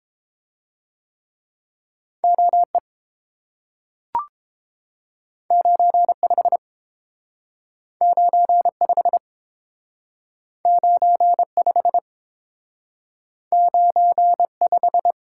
Morse code, heard as a keyed tone, about 30 wpm, 700 Hz. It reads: O E 9 5 9 5 9 5 9 5